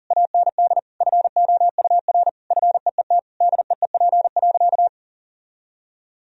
Morse code, heard as a keyed tone, about 40 wpm, 700 Hz. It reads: AND FOUR FEET DEEP.
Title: Sentences from Top 1000 Words - Review Part 3 - 40wpm